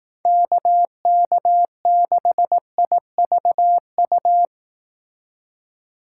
Morse code, heard as a keyed tone, about 18 wpm, 700 Hz. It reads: KK6IVU
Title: Speed Racing - USA 2x3 Call Signs 15wpm